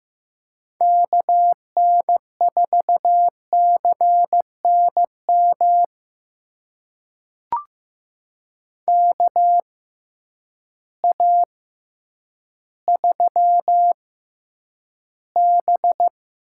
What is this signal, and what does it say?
Morse code, heard as a keyed tone, about 15 wpm, spaced as for 9 wpm, 700 Hz. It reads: KN4CNM KA3B
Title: USA Call Signs - 6 Characters - 6x Character Spacing - 15wpm